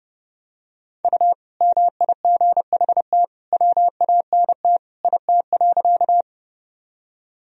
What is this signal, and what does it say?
Morse code, heard as a keyed tone, about 30 wpm, 700 Hz. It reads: U MIGHT WANT IT.
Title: Sentences from Top 300 Words and Abbreviations 30wpm